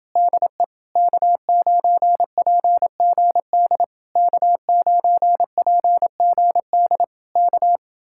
Morse code, heard as a keyed tone, about 28 wpm, 700 Hz. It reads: DE K9PGD K9PGD K